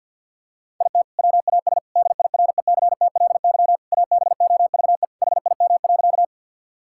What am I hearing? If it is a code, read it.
Morse code, heard as a keyed tone, about 50 wpm, 700 Hz. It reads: IT WAS DIRECTLY ABOVE HIM.
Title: Sentences from Top 1000 Words - Review Part 5 - 50wpm